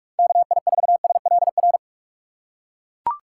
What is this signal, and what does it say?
Morse code, heard as a keyed tone, about 41 wpm, 700 Hz. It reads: KI4SLR
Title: USA Call Signs - 6 Characters - 4x Character Spacing - 45wpm